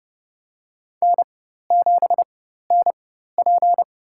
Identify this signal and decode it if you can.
Morse code, heard as a keyed tone, about 30 wpm, spaced as for 14 wpm, 700 Hz. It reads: N7NP
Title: USA Call Signs - 4 Characters - 4x Character Spacing - 30wpm